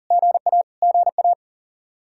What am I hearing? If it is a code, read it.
Morse code, heard as a keyed tone, about 40 wpm, 700 Hz. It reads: GA GA
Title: Practice QSO - Exchange 1 - Beginning - 40wpm